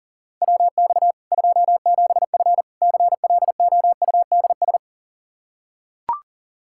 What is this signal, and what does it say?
Morse code, heard as a keyed tone, about 40 wpm, 700 Hz. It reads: WX 27F CLOUDS
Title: Practice QSO - Exchange 2 - Info - 40wpm